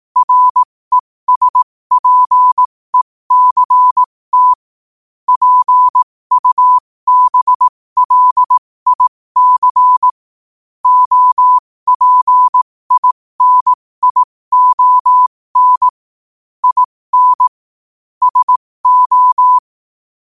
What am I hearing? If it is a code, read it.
Morse code, heard as a keyed tone, about 18 wpm, spaced as for 15 wpm, 1000 Hz. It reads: RESPECT PUBLIC OPINION IN SO
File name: qotdfiles / 15wpm